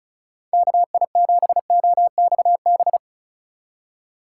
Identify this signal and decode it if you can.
Morse code, heard as a keyed tone, about 36 wpm, 700 Hz. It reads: KI7OXB
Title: USA Call Signs - 6 Characters - 3x Character Spacing - 35wpm